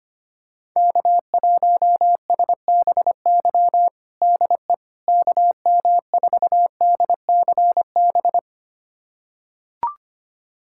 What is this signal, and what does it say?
Morse code, heard as a keyed tone, about 25 wpm, 700 Hz. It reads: K1SBY DE KM4DCB
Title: USA Callsign DE Callsign 25wpm